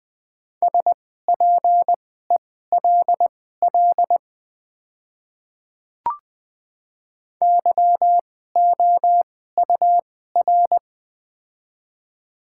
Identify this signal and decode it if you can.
Morse code, heard as a keyed tone, about 20 wpm, spaced as for 14 wpm, 700 Hz. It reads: SPELL YOUR
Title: 2x Character Spacing - Top 500 Words - 20wpm